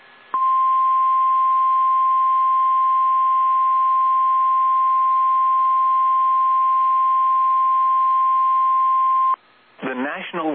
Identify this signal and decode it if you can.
Emergency Alert System (SAME) tones (the 1050 Hz attention tone), with speech. It is heard from 0.3 to 9.4 s.